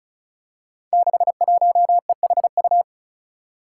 Morse code, heard as a keyed tone, about 34 wpm, 700 Hz. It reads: 61EHU
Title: Five Challenging Letter-Number Combinations 35wpm